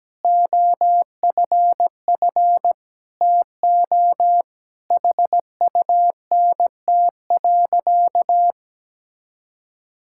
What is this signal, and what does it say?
Morse code, heard as a keyed tone, about 17 wpm, 700 Hz. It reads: OFF TO HUNT.
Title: Sentences from Top 1600 Words 17wpm